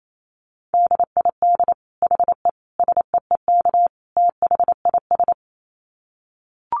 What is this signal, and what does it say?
Morse code, heard as a keyed tone, about 28 wpm, 700 Hz. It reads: DID HE SEEK THIS
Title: Dit Challenge 28wpm